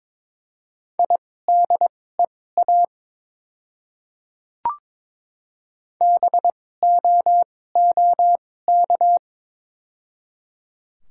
Morse code, heard as a keyed tone, about 22 wpm, spaced as for 16 wpm, 700 Hz. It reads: IDEA BOOK
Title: 2x Character Spacing - Top 300 Words - 22wpm